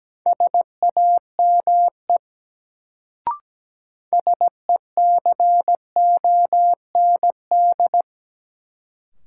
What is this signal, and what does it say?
Morse code, heard as a keyed tone, about 17 wpm, 700 Hz. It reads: SAME SECOND
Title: Sets of 2 Words - New Word Drill - 17wpm